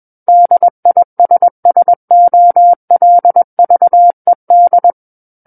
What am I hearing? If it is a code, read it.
Morse code, heard as a keyed tone, about 21 wpm, 700 Hz. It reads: DISSOLVED
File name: Book One / 12-Chapter 12 - 21 WPM